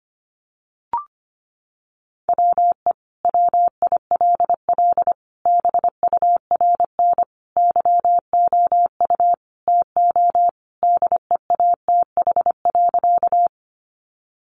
Morse code, heard as a keyed tone, about 25 wpm, 700 Hz. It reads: WE WILL BURN YOU TO DEATH.
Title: Sentences from Top 1300 Words 25wpm